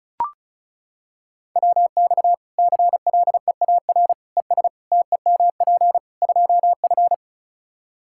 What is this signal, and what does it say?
Morse code, heard as a keyed tone, about 35 wpm, 700 Hz. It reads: WX CLEAR ES TEMP 2F
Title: Practice QSO - Exchange 2 - Info - 35wpm